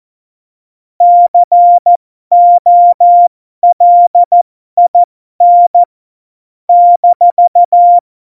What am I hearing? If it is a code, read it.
Morse code, heard as a keyed tone, about 14 wpm, spaced as for 12 wpm, 700 Hz. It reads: COLIN -
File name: Morsekurs / Witze